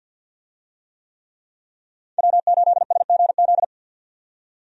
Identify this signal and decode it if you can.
Morse code, heard as a keyed tone, about 49 wpm, 700 Hz. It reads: W8SGZ
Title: USA Call Signs - 5 Characters - 50wpm